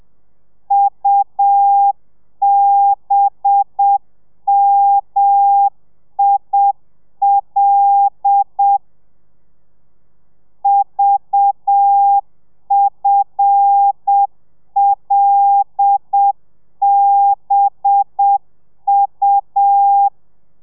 Morse code, heard as a keyed tone, about 7 wpm, 800 Hz. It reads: UBMIL VFLBU
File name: telegrafia / i2tfj / Lez5 30